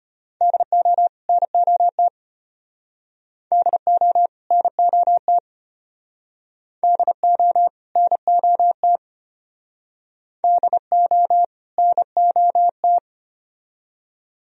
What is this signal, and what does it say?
Morse code, heard as a keyed tone, about 33 wpm, 700 Hz. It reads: DO NOT DO NOT DO NOT DO NOT